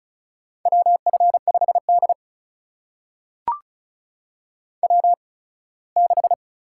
Morse code, heard as a keyed tone, about 36 wpm, spaced as for 21 wpm, 700 Hz. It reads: WF5D W6
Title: USA Call Signs - 4 Characters - 8x Character Spacing - 35wpm